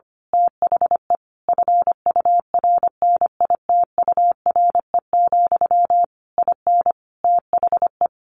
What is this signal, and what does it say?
Morse code, heard as a keyed tone, about 25 wpm, 700 Hz. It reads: THE FURNITURE, IN THE
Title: A Sound Of Thunder 25wpm